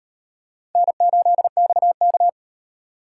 Morse code, heard as a keyed tone, about 35 wpm, 700 Hz. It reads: N8XK